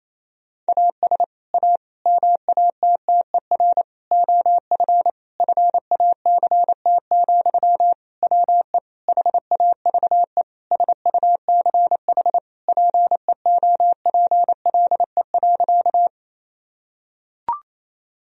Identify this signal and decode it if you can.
Morse code, heard as a keyed tone, about 28 wpm, 700 Hz. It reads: AS A MATTER OF FACT, WE HAVE SUCH PEOPLE.
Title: Sentences from Top 1000 Words - Review Part 3 - 28wpm